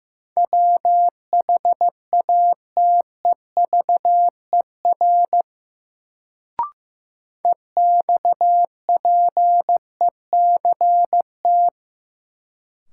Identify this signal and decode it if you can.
Morse code, heard as a keyed tone, about 15 wpm, 700 Hz. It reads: WHATEVER EXPECT
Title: Sentences from Top 1100 Words - New Word Drill - 15wpm